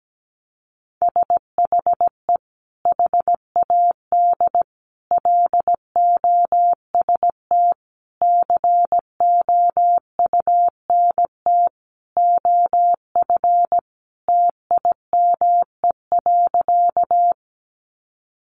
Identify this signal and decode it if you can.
Morse code, heard as a keyed tone, about 17 wpm, 700 Hz. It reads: SHE HAD LOST COUNT OF TIME.
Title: Sentences from Top 1500 Words 17wpm